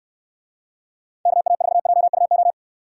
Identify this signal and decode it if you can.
Morse code, heard as a keyed tone, about 65 wpm, 700 Hz. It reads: KA4JUO